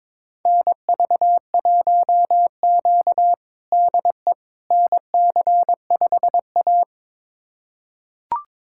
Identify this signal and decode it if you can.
Morse code, heard as a keyed tone, about 22 wpm, 700 Hz. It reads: NV1Q DE NC5A
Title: USA Callsign DE Callsign - 4 Characters - 22wpm